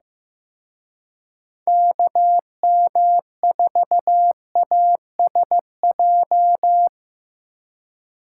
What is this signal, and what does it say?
Morse code, heard as a keyed tone, about 18 wpm, 700 Hz. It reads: KM4ASJ
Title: Speed Racing - USA 2x3 Call Signs 15wpm